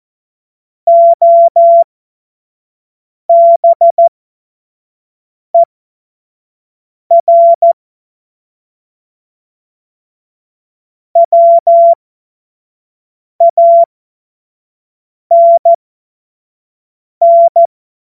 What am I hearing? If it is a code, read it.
Morse code, heard as a keyed tone, about 14 wpm, spaced as for 5 wpm, 680 Hz. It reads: OBER WANN
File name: Morsekurs / Witze / Witz 29. Juni 2021, 25 F70